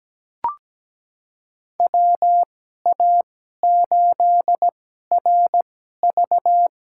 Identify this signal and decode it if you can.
Morse code, heard as a keyed tone, about 17 wpm, spaced as for 12 wpm, 700 Hz. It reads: WA8RV
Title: USA Call Signs - 2x Character Spacing - 17wpm